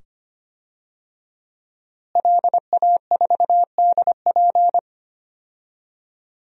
Morse code, heard as a keyed tone, about 25 wpm, 700 Hz. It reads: LA4DP
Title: DXCC-100 Call Signs - 5 Characters - Repeated 2x - 25wpm